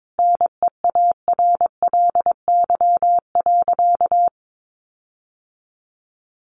Morse code, heard as a keyed tone, about 22 wpm, 700 Hz. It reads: NEARLY.
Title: Sentences from Top 1300 Words 22wpm